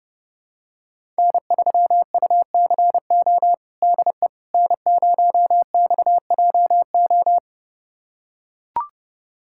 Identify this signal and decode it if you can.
Morse code, heard as a keyed tone, about 30 wpm, 700 Hz. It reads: N3UCO DE N0XJO
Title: USA Callsign DE Callsign - 5 Characters - 30wpm